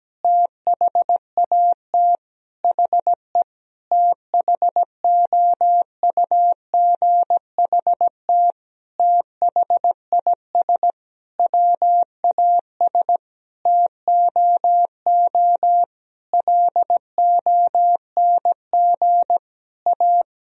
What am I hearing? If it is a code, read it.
Morse code, heard as a keyed tone, about 17 wpm, 700 Hz. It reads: THAT HE THOUGHT THIS WAS TOO LONG A
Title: Sentences from Top 1000 Words - Review Part 4 - 17wpm